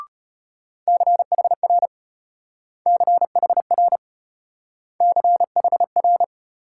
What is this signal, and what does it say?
Morse code, heard as a keyed tone, about 32 wpm, 700 Hz. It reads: CHR CHR CHR